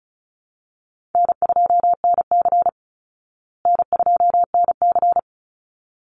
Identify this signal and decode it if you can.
Morse code, heard as a keyed tone, about 34 wpm, 700 Hz. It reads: N2NC N2NC
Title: Easy Call Signs Repeated 2x - 35wpm